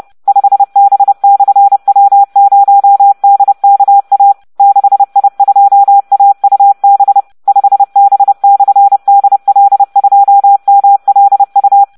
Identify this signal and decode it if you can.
Morse code, heard as a keyed tone, about 30 wpm, 800 Hz. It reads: 5B/W0DKA 6I2AUB 5B/DL2MLU